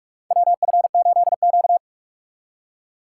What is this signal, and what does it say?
Morse code, heard as a keyed tone, about 43 wpm, 700 Hz. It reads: WF8Q
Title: USA Call Signs - 4 Characters - 2x Character Spacing - 45wpm